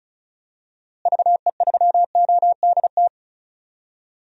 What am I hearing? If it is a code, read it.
Morse code, heard as a keyed tone, about 36 wpm, 700 Hz. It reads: VE3ODT
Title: DXCC-100 Call Signs - 6 Characters - Repeated 3x - 35wpm